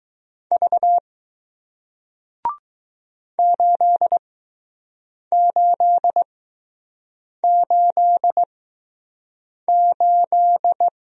Morse code, heard as a keyed tone, about 20 wpm, 700 Hz. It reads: V 8 8 8 8